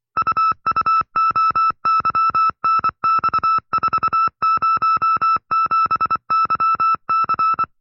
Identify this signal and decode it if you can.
Morse code, heard as a keyed tone, about 24 wpm, 1310 Hz. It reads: UUOYNX407YC
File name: FreeFiles / CW-practice / 20WPM-side2